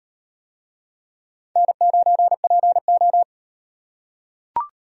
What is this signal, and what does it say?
Morse code, heard as a keyed tone, about 38 wpm, 700 Hz. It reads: N9PO